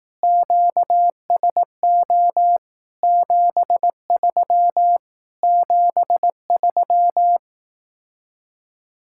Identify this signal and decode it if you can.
Morse code, heard as a keyed tone, about 18 wpm, 700 Hz. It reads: QSO 73 73
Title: Speed-Racing Common QSO Phrases Encore - 15wpm